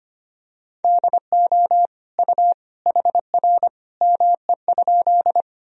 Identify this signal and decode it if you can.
Morse code, heard as a keyed tone, about 25 wpm, 700 Hz. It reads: DO U HR ME?
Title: Sentences from Top 300 Words and Abbreviations 25wpm